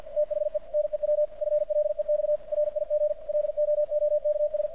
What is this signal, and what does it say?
Morse code, heard as a keyed tone, about 50 wpm, 600 Hz, with radio noise. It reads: THE NEW FLEX RIG LOOKS
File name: archived / CRSnet